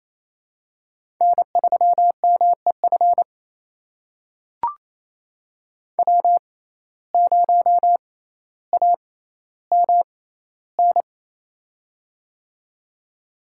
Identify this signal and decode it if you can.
Morse code, heard as a keyed tone, about 27 wpm, spaced as for 14 wpm, 700 Hz. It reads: N3MEF W0AMN
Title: USA Call Signs - 5 Characters - 6x Character Spacing - 28wpm